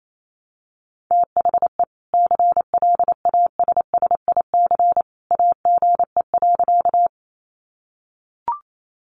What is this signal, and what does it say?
Morse code, heard as a keyed tone, about 28 wpm, 700 Hz. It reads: THE CLASSIC AGE.
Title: Sentences from Top 1000 Words 28wpm